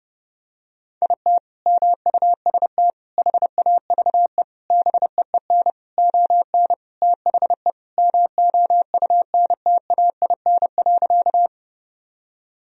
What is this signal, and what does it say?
Morse code, heard as a keyed tone, about 30 wpm, 700 Hz. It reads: IT MUST HAVE BEEN ON THE MOUNTAIN.
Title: Sentences from Top 1000 Words - Review Part 1 - 30wpm